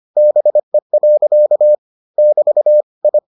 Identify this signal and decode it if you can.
Morse code, heard as a keyed tone, about 25 wpm, 600 Hz. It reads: BE. <BT> I